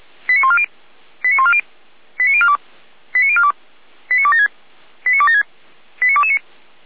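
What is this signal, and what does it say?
Five-tone selective call (ZVEI), heard as quick runs of five tones, tones of about 70 ms. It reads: 89140 89140 89042 89042 89276 89276 89229
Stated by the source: ZVEI five-tone call received by SDR.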